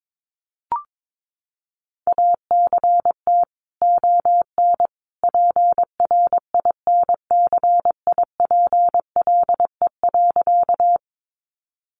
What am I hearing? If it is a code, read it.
Morse code, heard as a keyed tone, about 22 wpm, 700 Hz. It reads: ACT ON PRINCIPLE.
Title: Sentences from Top 1400 Words 22wpm